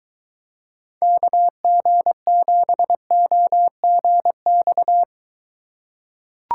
Morse code, heard as a keyed tone, about 23 wpm, 700 Hz. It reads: KG7OGX